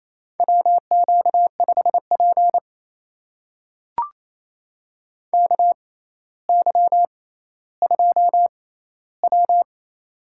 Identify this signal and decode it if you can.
Morse code, heard as a keyed tone, about 28 wpm, spaced as for 14 wpm, 700 Hz. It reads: WQ5P KY2W
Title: USA Call Signs - 6x Character Spacing - 28wpm